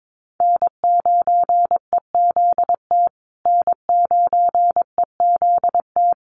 Morse code, heard as a keyed tone, about 22 wpm, 700 Hz. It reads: N9EZT N9EZT
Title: USA Call Signs Repeated 3x 22wpm